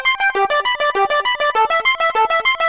There is no code.